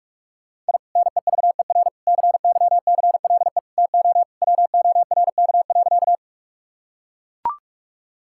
Morse code, heard as a keyed tone, about 41 wpm, 700 Hz. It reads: I NEVER CYCLE TO WORK.
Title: Sentences from Top 1000 Words - Review Part 6 - 45wpm